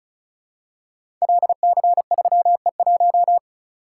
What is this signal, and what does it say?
Morse code, heard as a keyed tone, about 34 wpm, 700 Hz. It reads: LC3E1
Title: Code Groups - 5 Letter:Numbers - 35wpm